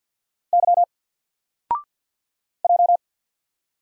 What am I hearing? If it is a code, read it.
Morse code, heard as a keyed tone, about 50 wpm, 700 Hz. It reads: Y J
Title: Single Letter-Number 50wpm